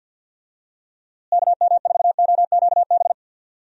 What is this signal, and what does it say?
Morse code, heard as a keyed tone, about 51 wpm, 700 Hz. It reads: KM4OQB
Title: USA Call Signs - 6 Characters - Repeated 2x - 50wpm